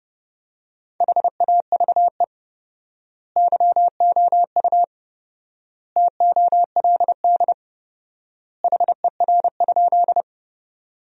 Morse code, heard as a keyed tone, about 30 wpm, 700 Hz. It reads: HAVE YOU TOLD HER?
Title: 4x Word-Spacing - Sentences from Top 400 Words - 30wpm